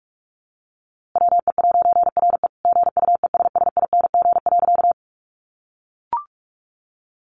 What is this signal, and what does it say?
Morse code, heard as a keyed tone, about 44 wpm, 700 Hz. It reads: WE'RE GUESSING.
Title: A Sound Of Thunder 45wpm